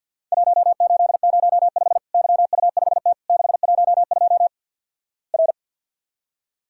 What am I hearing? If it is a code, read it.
Morse code, heard as a keyed tone, about 50 wpm, 700 Hz. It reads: 1805 YU5T 612 R
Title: Worked All Europe DX Contest - QTC Exchange - 50wpm